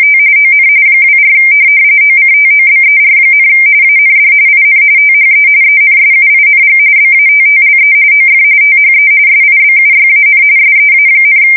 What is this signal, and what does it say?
RTTY, about 45.45 baud. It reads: T ON 24 OCTOBER, AND IN THE ENSUING BATTLE OF THE SIBUYAN SEA,
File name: fsk170 / USS-Massachusetts